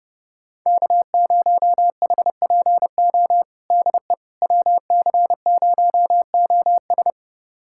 Morse code, heard as a keyed tone, about 30 wpm, 700 Hz. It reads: K0HPO DE WC0OS
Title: USA Callsign DE Callsign - 5 Characters - 30wpm